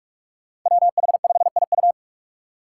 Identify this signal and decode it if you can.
Morse code, heard as a keyed tone, about 44 wpm, 700 Hz. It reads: WH5IU
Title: Five Challenging Letter-Number Combinations 45wpm